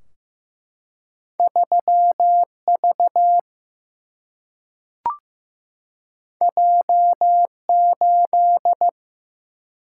Morse code, heard as a keyed tone, about 15 wpm, 700 Hz. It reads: 3V J8
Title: Single-Letter Single-Number 15wpm